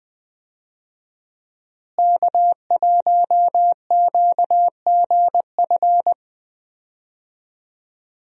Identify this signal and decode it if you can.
Morse code, heard as a keyed tone, about 20 wpm, 700 Hz. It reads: K1QGF
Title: USA Call Signs - 5 Characters - Repeated 3x - 20wpm